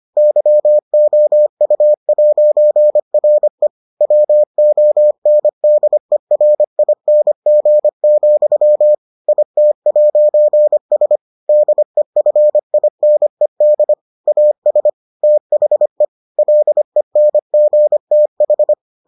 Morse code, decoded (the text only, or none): YOU'RE WONDERING, IT'S DEFINED AS THE LENGTH